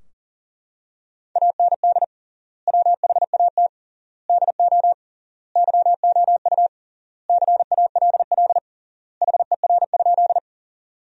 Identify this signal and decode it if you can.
Morse code, heard as a keyed tone, about 40 wpm, 700 Hz. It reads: AND WHAT DO YOU CALL HER?
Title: 3x Word-Spacing - Sentences from Top 100 Words - 40wpm